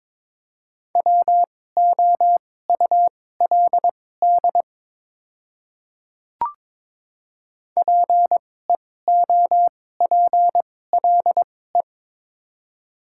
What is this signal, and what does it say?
Morse code, heard as a keyed tone, about 22 wpm, spaced as for 16 wpm, 700 Hz. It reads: WOULD PEOPLE
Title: 2x Character Spacing - Top 200 Words - 22wpm